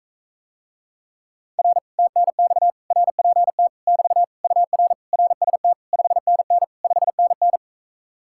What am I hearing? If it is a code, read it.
Morse code, heard as a keyed tone, about 44 wpm, 700 Hz. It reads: R TNX RPT <BT> UR RST 5NN 5NN